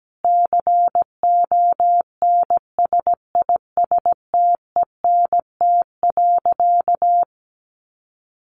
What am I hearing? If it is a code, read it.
Morse code, heard as a keyed tone, about 17 wpm, 700 Hz. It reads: CONSISTENT.
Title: Sentences from Top 1500 Words 17wpm